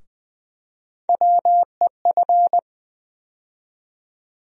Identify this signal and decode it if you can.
Morse code, heard as a keyed tone, about 20 wpm, 700 Hz. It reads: WEF
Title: USA Call Sign Postfixes - 3 Letters 20wpm